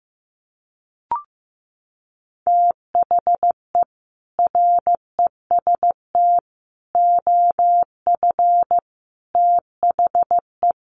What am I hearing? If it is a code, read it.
Morse code, heard as a keyed tone, about 15 wpm, 700 Hz. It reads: THE REST OF THE
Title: Sets of 5 Words 15wpm